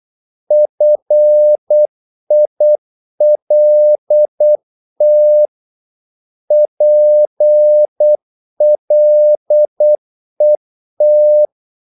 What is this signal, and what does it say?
Morse code, heard as a keyed tone, about 8 wpm, 600 Hz. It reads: FILT PLET
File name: Korte ord DK 8wpm